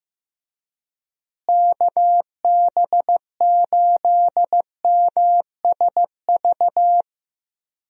Morse code, heard as a keyed tone, about 16 wpm, 700 Hz. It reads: KB8MSV
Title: USA Call Signs - 6 Characters - Repeated 2x - 15wpm